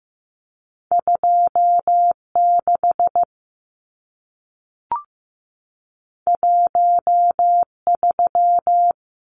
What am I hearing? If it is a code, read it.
Morse code, heard as a keyed tone, about 15 wpm, 700 Hz. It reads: 26 13
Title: Numbers - 2-Digit 15wpm